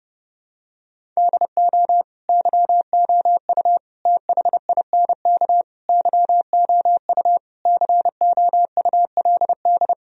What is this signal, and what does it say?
Morse code, heard as a keyed tone, about 29 wpm, 700 Hz. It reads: DO YOU THINK YOU COULD
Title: Sentences from Top 1000 Words - Review Part 1 - 30wpm